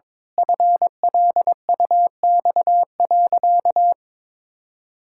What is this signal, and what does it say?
Morse code, heard as a keyed tone, about 22 wpm, 700 Hz. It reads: FLUX.
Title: Pangram Challenge v2 - 22wpm